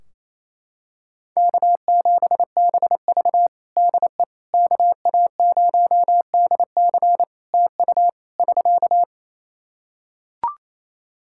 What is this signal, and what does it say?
Morse code, heard as a keyed tone, about 27 wpm, 700 Hz. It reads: K7BV DE KA0DC TU <SK>
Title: Practice QSO - Exchange 4 - Ending - 28wpm